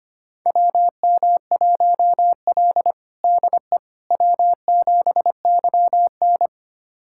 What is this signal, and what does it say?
Morse code, heard as a keyed tone, about 25 wpm, 700 Hz. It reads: WM1L DE W7YN
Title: USA Callsign DE Callsign - 4 Characters - 25wpm